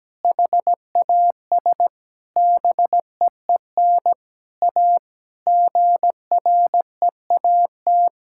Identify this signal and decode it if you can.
Morse code, heard as a keyed tone, about 17 wpm, 700 Hz. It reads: HAS BEEN A GREAT
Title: Sentences from Top 1000 Words - Review Part 6 - 17wpm